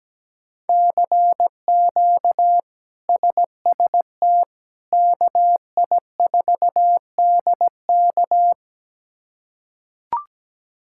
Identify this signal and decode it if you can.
Morse code, heard as a keyed tone, about 17 wpm, 700 Hz. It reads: CQ SST KI4DK
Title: K1USN SST QSO Practice - 17wpm